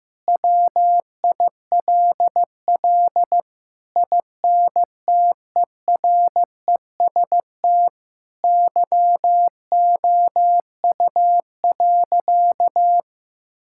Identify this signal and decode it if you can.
Morse code, heard as a keyed tone, about 16 wpm, 700 Hz. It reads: WILL INTEREST YOU.
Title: Sentences from Top 1000 Words - Review Part 5 - 15wpm